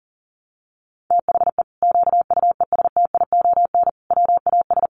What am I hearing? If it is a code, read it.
Morse code, heard as a keyed tone, about 40 wpm, 700 Hz. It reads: THE QUESTION WAS
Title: Sentences from Top 1200 Words 40wpm